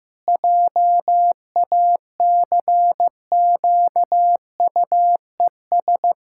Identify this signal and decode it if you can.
Morse code, heard as a keyed tone, about 15 wpm, 700 Hz. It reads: JACQUES
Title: Pangram Challenge v2 - 15wpm